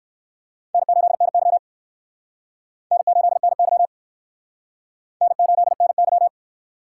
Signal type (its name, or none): Morse code